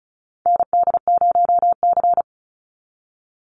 Morse code, heard as a keyed tone, about 34 wpm, 700 Hz. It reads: ND0C
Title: NA SOTA Chaser Callsigns 35wpm